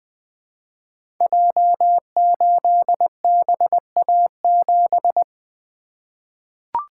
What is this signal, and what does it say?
Morse code, heard as a keyed tone, about 20 wpm, 700 Hz. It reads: J8BA7
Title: Code Groups - 5 Letter:Numbers - 20wpm